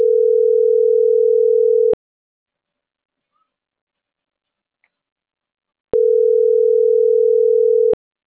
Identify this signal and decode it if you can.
Ringback tone (North America), 440 + 480 Hz.